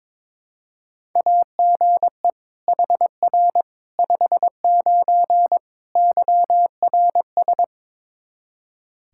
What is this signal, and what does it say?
Morse code, heard as a keyed tone, about 22 wpm, 700 Hz. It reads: AGE HR 59 YRS
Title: Practice QSO - Exchange 3 - More Info - 22wpm